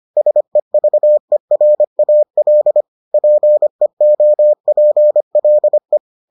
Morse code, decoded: SEVERAL PEOPLE

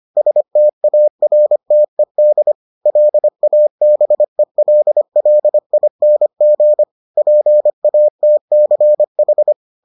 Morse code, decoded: STARTED LABELLING PATCH

